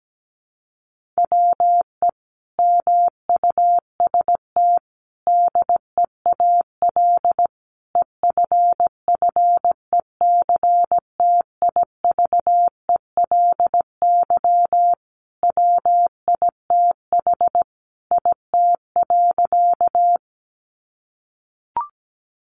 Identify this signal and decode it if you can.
Morse code, heard as a keyed tone, about 17 wpm, 700 Hz. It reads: WE MUST DEAL EFFECTIVELY WITH IT.
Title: Sentences from Top 1400 Words 17wpm